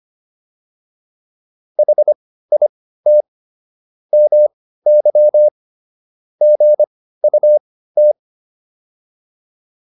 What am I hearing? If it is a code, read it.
Morse code, heard as a keyed tone, about 25 wpm, spaced as for 15 wpm, 600 Hz. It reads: HIT MY GUT